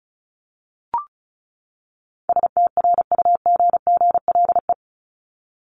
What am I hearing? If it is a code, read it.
Morse code, heard as a keyed tone, about 34 wpm, 700 Hz. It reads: STRUGGLE
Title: Top 1751-2000 Words 35wpm